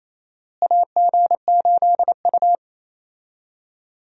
Morse code, heard as a keyed tone, about 27 wpm, 700 Hz. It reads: AG8U
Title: USA Call Signs - 4 Characters - Repeated 3x - 28wpm